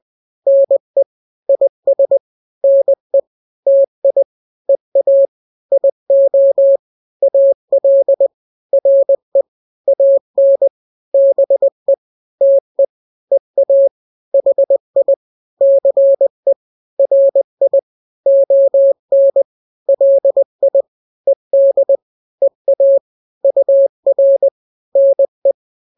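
Morse code, heard as a keyed tone, about 20 wpm, 550 Hz. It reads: NE IS NE TI EA IO AL RE AN BE TE EA HI CE RI ON LI ED EA UR NE
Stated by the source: Morse drill of two-letter n-grams at 20/18 WPM (Farnsworth), from KK7RR.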